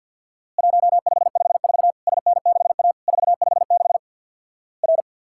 Morse code, heard as a keyed tone, about 50 wpm, 700 Hz. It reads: 1554 SN6A 456 R
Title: Worked All Europe DX Contest - QTC Exchange with Repeats - 50wpm